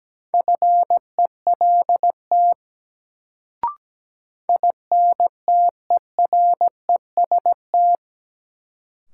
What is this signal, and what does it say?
Morse code, heard as a keyed tone, about 17 wpm, 700 Hz. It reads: FELT INTEREST
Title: Sets of 5 Words Encore - New Word Drill - 17wpm